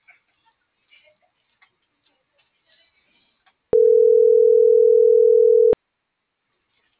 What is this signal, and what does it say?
Ringback tone (North America), 440 + 480 Hz.